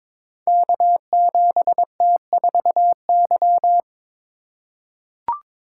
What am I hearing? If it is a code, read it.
Morse code, heard as a keyed tone, about 22 wpm, 700 Hz. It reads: K7T4Y
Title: Code Groups - 5 Letter:Numbers - 22wpm